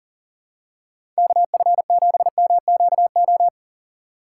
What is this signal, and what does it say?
Morse code, heard as a keyed tone, about 40 wpm, 700 Hz. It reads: KF7MQO
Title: USA Call Signs - 6 Characters - 3x Character Spacing - 40wpm